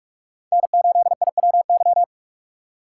Morse code, heard as a keyed tone, about 46 wpm, 700 Hz. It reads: N8IWY
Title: K1USN SST QSO Practice - 45wpm